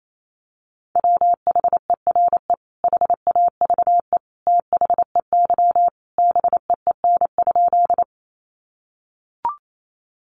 Morse code, heard as a keyed tone, about 28 wpm, 700 Hz. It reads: WHERE HAVE THEY BEEN?
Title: Sentences from Top 200 Words 28wpm